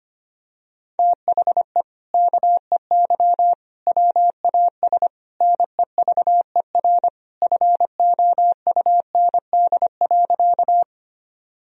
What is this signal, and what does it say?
Morse code, heard as a keyed tone, about 25 wpm, 700 Hz. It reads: THE KEY WAS NEVER FOUND.
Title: Sentences from Top 1000 Words - Review Part 4 - 25wpm